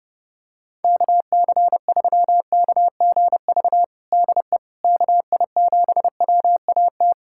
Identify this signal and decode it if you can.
Morse code, heard as a keyed tone, about 30 wpm, 700 Hz. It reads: KC3KGV DE KI7WAT
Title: USA Callsign DE Callsign 30wpm